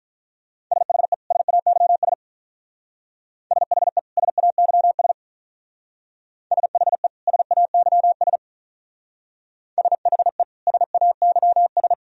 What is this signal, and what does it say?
Morse code, heard as a keyed tone, about 42 wpm, 700 Hz. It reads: SHE SAYS SHE SAYS SHE SAYS SHE SAYS
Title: Speed-Racing - Sets of 2 Words 35wpm